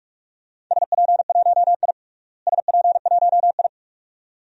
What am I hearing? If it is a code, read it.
Morse code, heard as a keyed tone, about 46 wpm, 700 Hz. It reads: SP1I SP1I
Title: DXCC-100 Call Signs - 4 Characters - Repeated 2x - 45wpm